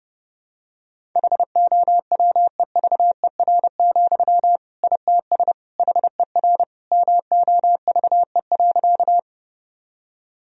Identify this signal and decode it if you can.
Morse code, heard as a keyed tone, about 30 wpm, 700 Hz. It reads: HOWEVER, ITS HER MOVE.
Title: Sentences from Top 1000 Words - Review Part 1 - 30wpm